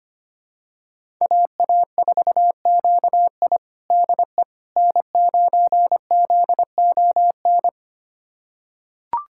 Morse code, heard as a keyed tone, about 25 wpm, 700 Hz. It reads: AA4QI DE N9ZON
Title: USA Callsign DE Callsign - 5 Characters - 25wpm